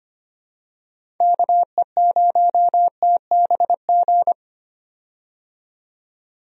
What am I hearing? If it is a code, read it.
Morse code, heard as a keyed tone, about 25 wpm, 700 Hz. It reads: KE0TBG
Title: USA Call Signs - 6 Characters - 25wpm